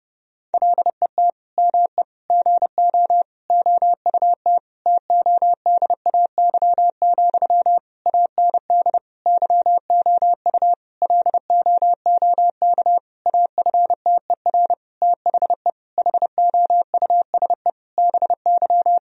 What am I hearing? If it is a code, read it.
Morse code, heard as a keyed tone, about 30 wpm, 700 Hz. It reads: LET ME GO OUT TODAY, AND YOU LOOK AFTER THE HOUSE BY
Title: Sentences from Top 1000 Words - Review Part 4 - 30wpm